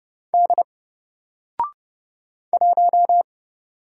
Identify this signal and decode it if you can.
Morse code, heard as a keyed tone, about 30 wpm, 700 Hz. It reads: D 1
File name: Single-Letter-Number-30wpm